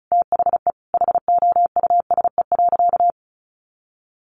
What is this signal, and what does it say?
Morse code, heard as a keyed tone, about 34 wpm, 700 Hz. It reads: THE HOUSE.
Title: Sentences from Top 400 Words 35wpm